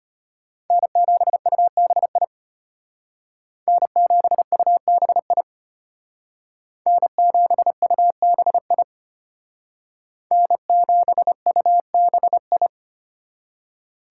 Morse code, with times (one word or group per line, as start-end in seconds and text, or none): 0.7-2.2 N7UBI
3.7-5.4 N7UBI
6.9-8.8 N7UBI
10.3-12.7 N7UBI